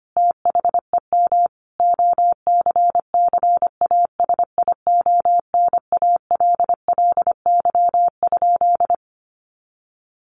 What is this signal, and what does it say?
Morse code, heard as a keyed tone, about 25 wpm, 700 Hz. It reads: THEM OCCASIONALLY?
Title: Sentences from Top 1300 Words 25wpm